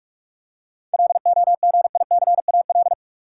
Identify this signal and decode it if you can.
Morse code, heard as a keyed tone, about 46 wpm, 700 Hz. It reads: LOGICAL
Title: Seven Letter Words - 45wpm